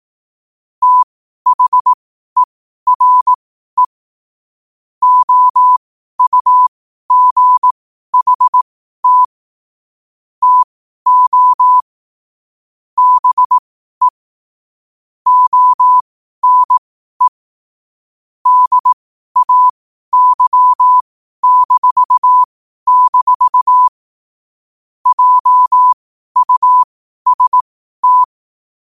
Morse code, heard as a keyed tone, about 18 wpm, spaced as for 12 wpm, 1000 Hz. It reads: THERE OUGHT TO BE ONE DAY-- JUST
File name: qotdfiles / 12wpm